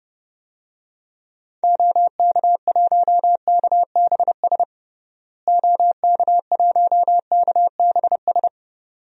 Morse code, heard as a keyed tone, about 30 wpm, 700 Hz. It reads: OK1KBS OK1KBS
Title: DXCC-100 Call Signs - 6 Characters - Repeated 2x - 30wpm